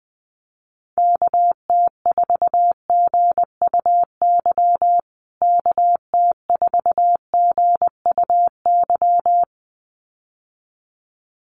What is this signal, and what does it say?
Morse code, heard as a keyed tone, about 20 wpm, 700 Hz. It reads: KT4GUY KT4GUY
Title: USA Call Signs Repeated 2x 20wpm